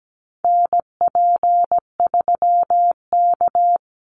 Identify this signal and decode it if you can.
Morse code, heard as a keyed tone, about 17 wpm, 700 Hz. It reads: NP3K
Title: Call Signs Repeated 3x - 17wpm